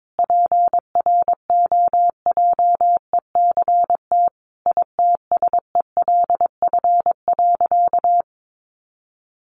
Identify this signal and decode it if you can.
Morse code, heard as a keyed tone, about 22 wpm, 700 Hz. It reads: PROJECT ITSELF.
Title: Sentences from Top 1400 Words 22wpm